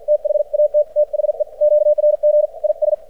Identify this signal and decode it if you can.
Morse code, heard as a keyed tone, about 54 wpm, spaced as for 47 wpm, 600 Hz. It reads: THAT THE QRG IS
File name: archived / CRSnet